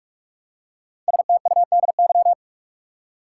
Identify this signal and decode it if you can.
Morse code, heard as a keyed tone, about 41 wpm, 700 Hz. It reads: STUDY
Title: Five Letter Words - 45wpm